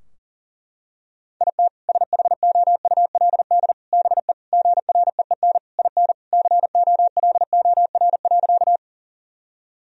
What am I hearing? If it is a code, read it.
Morse code, heard as a keyed tone, about 40 wpm, 700 Hz. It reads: IT SHOULD BE GREEN IN COLOR.
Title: Sentences from Top 1000 Words - Review Part 2 - 40wpm